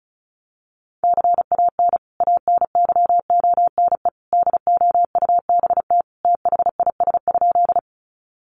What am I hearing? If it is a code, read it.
Morse code, heard as a keyed tone, about 34 wpm, 700 Hz. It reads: CAN ANYONE DOUBT THIS?
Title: Sentences from Top 1500 Words 35wpm